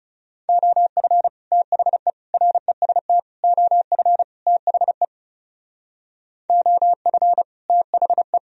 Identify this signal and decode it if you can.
Morse code, heard as a keyed tone, about 39 wpm, 700 Hz. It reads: OF THE REST OF THE OF THE